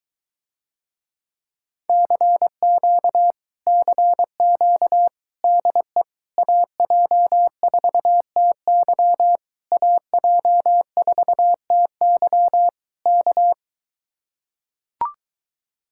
Morse code, heard as a keyed tone, about 24 wpm, 700 Hz. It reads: CQ CQ DE AJ4TY AJ4TY K